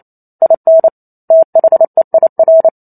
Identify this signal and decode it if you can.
Morse code, heard as a keyed tone, about 28 wpm, 650 Hz. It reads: IN THEIR
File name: Chapter 2 - 28 WPM